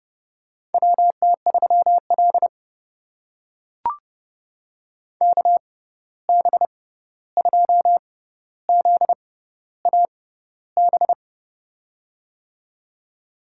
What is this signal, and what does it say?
Morse code, heard as a keyed tone, about 31 wpm, spaced as for 14 wpm, 700 Hz. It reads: WT3L KB2ZAB
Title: USA Call Signs - 6x Character Spacing - 30wpm